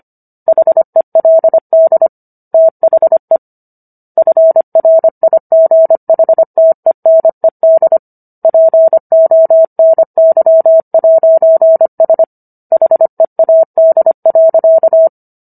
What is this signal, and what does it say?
Morse code, heard as a keyed tone, about 25 wpm, 650 Hz. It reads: HELD THE FRIGHTENED PONY'S HEAD.